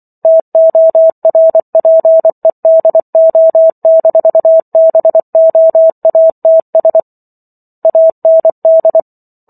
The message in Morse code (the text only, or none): TORPEDO-BOATS AND